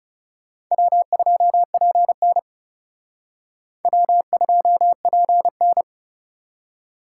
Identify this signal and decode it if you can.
Morse code, heard as a keyed tone, about 38 wpm, 700 Hz. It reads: W2PN W2PN